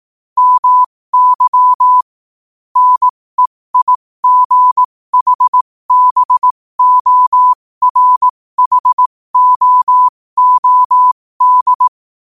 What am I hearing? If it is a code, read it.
Morse code, heard as a keyed tone, about 18 wpm, spaced as for 15 wpm, 1000 Hz. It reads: MY NEIGHBORHOOD
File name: qotdfiles / 15wpm